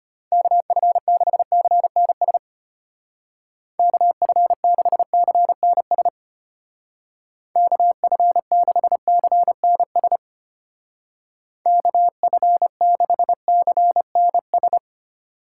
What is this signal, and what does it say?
Morse code, heard as a keyed tone, about 31 wpm, 700 Hz. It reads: KF6CNS KF6CNS KF6CNS KF6CNS